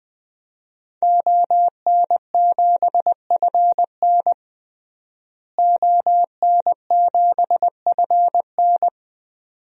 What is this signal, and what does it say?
Morse code, heard as a keyed tone, about 20 wpm, 700 Hz. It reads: ON7FN ON7FN
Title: DXCC-100 Call Signs - 5 Characters - Repeated 2x - 20wpm